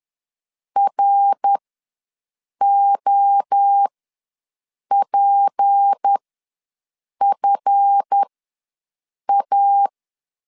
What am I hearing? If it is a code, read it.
Morse code, heard as a keyed tone, about 10 wpm, spaced as for 6 wpm, 800 Hz. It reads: ROPFA
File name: LEZIONI / 10 SIGNS / 10D COMA ws 10wpm 4314